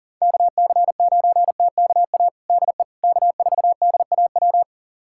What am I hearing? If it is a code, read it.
Morse code, heard as a keyed tone, about 40 wpm, 700 Hz. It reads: KC9TKA DE K4DAW
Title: USA Callsign DE Callsign 40wpm